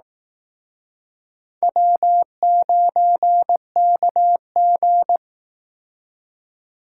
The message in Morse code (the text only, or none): W9KG